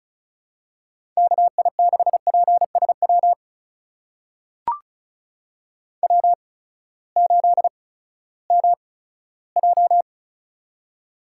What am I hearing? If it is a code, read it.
Morse code, heard as a keyed tone, about 36 wpm, spaced as for 18 wpm, 700 Hz. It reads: KI6PSW W8MJ
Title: USA Call Signs - 8x Character Spacing - 35wpm